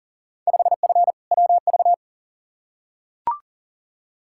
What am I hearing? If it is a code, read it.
Morse code, heard as a keyed tone, about 40 wpm, 700 Hz. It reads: 5F WV
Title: Field Day Exchanges Encore - 40wpm